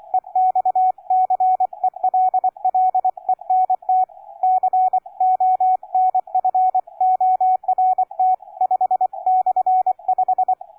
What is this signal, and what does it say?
Morse code, heard as a keyed tone, about 24 wpm, 740 Hz, with radio noise. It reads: EXCELLENT CONFORT 5/5
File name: cw forum / K9DX 1818 le 17-12-2008 05h47